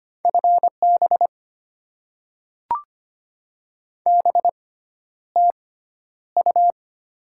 Morse code, heard as a keyed tone, about 25 wpm, spaced as for 11 wpm, 690 Hz. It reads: FB BTU
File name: Top 20 QSO Elements - 6x Character Spacing - 25wpm